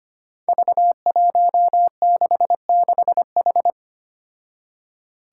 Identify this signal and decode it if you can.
Morse code, heard as a keyed tone, about 25 wpm, 700 Hz. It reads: V166H
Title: Five Challenging Letter-Number Combinations 25wpm